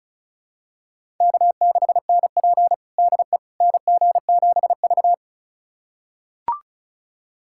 Morse code, heard as a keyed tone, about 36 wpm, 700 Hz. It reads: K6NP DE NG7V
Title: USA Callsign DE Callsign - 4 Characters - 35wpm